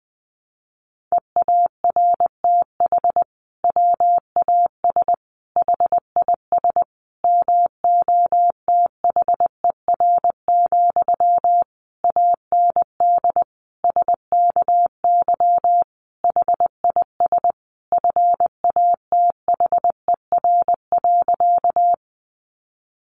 Morse code, heard as a keyed tone, about 20 wpm, 700 Hz. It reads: EARTH WAS HIS MOTHER, AND SKY HIS FATHER.
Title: Sentences from Top 1200 Words 20wpm